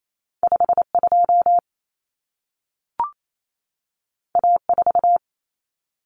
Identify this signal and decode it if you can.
Morse code, heard as a keyed tone, about 28 wpm, 700 Hz. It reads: 52 A4
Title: Common Cut Numbers - 2-Digit 28wpm